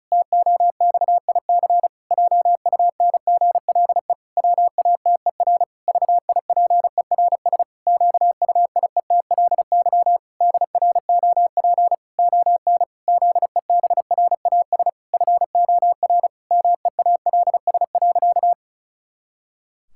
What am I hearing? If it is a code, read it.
Morse code, heard as a keyed tone, about 36 wpm, 700 Hz. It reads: TOXIC JUNGLE WATER VIPERS QUIETLY DROP ON ZEBRAS FOR MEALS.
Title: Pangram Challenge v2 - 35wpm